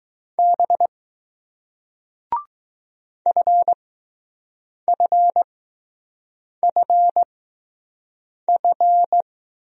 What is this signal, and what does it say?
Morse code, heard as a keyed tone, about 20 wpm, spaced as for 6 wpm, 700 Hz. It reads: B FFFF